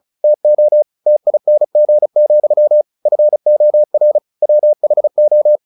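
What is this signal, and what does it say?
Morse code, heard as a keyed tone, about 34 wpm, 600 Hz. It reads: TO TING, FOR WHO
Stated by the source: Morse practice at 35 wpm from CW med Gnister, sending Ken Follett's novel Never in English.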